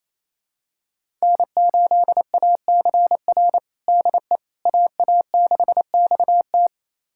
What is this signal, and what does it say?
Morse code, heard as a keyed tone, about 28 wpm, 700 Hz. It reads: N8ACR DE AA6XT
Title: USA Callsign DE Callsign 28wpm